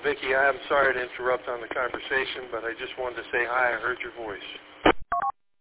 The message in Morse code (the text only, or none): none